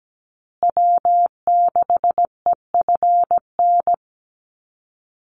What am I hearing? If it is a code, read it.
Morse code, heard as a keyed tone, about 17 wpm, 700 Hz. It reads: W6EFN
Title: USA Call Signs Repeated 3x 17wpm